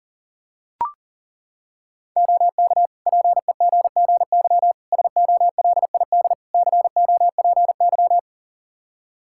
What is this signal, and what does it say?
Morse code, heard as a keyed tone, about 40 wpm, 700 Hz. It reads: OK PEGGY SOLID COPY
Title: Practice QSO - Exchange 3 - More Info - 40wpm